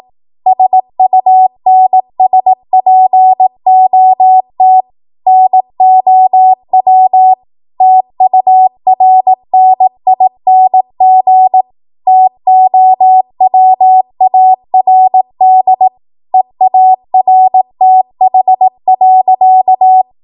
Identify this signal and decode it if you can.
Morse code, heard as a keyed tone, about 18 wpm, 750 Hz. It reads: SUNSPOT NOW TURNING TOWARD EARTH.
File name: Morse / 18WPM